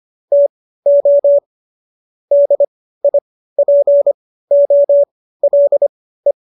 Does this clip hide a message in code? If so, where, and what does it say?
Yes: Morse code from 0.3 to 6.3 s, heard as a keyed tone, about 25 wpm, spaced as for 15 wpm, 570 Hz. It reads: TO DIPOLE